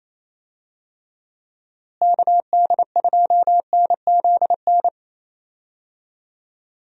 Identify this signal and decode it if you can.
Morse code, heard as a keyed tone, about 27 wpm, 700 Hz. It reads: KD2NZN
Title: USA Call Signs - 6 Characters - Repeated 2x - 28wpm